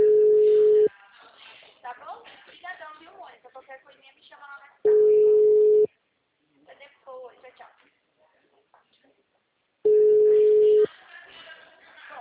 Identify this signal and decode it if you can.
Ringback tone (Europe, 425 Hz), with speech.
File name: out- -2007- - - .36078